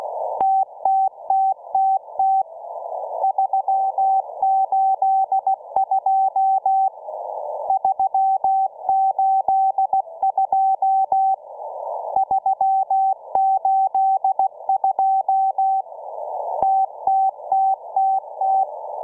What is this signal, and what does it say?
Morse code, heard as a keyed tone, about 16 wpm, 770 Hz. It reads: TTTTT 382 382 382 TTTTT